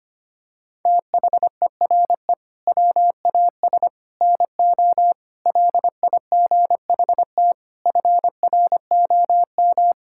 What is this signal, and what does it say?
Morse code, heard as a keyed tone, about 25 wpm, 700 Hz. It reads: THERE WAS NO LIGHT FROM
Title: Sentences from Top 1000 Words - Review Part 5 - 25wpm